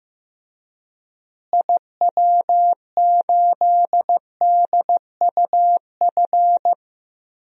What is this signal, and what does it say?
Morse code, heard as a keyed tone, about 15 wpm, 700 Hz. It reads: IW8DUF
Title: DXCC-100 Call Signs - 6 Characters - Repeated 2x - 15wpm